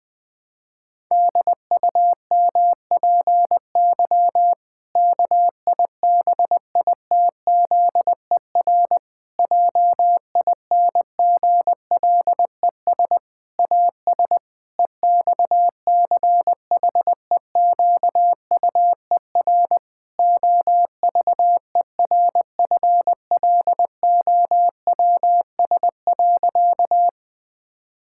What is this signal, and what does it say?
Morse code, heard as a keyed tone, about 20 wpm, 700 Hz. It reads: DUMPY KIBITZER JINGLES AS EXCHEQUER OVERFLOWS.
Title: Pangram Challenge v2 - 20wpm